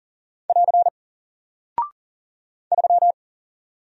Morse code, heard as a keyed tone, about 40 wpm, 700 Hz. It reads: <AR> 3
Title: Single Character (Letter:Number:Punctuation:Prosign) Alt 40wpm